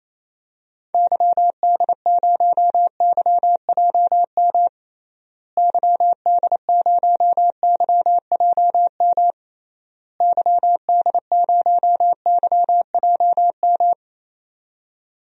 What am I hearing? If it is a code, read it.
Morse code, heard as a keyed tone, about 27 wpm, 700 Hz. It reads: YD0YJM YD0YJM YD0YJM
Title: DXCC-100 Call Signs - 6 Characters - Repeated 3x - 28wpm